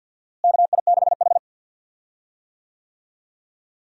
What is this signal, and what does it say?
Morse code, heard as a keyed tone, about 47 wpm, 700 Hz. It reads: KI6H
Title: USA Call Signs - 4 Characters - Repeated 3x - 50wpm